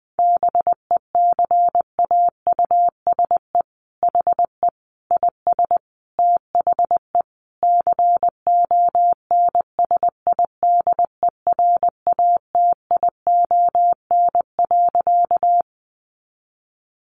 Morse code, heard as a keyed tone, about 20 wpm, 700 Hz. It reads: BECAUSE HE IS THE CONSIDERATION.
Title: Sentences from Top 1500 Words 20wpm